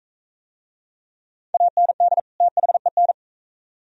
Morse code, heard as a keyed tone, about 37 wpm, 690 Hz. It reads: AND THEN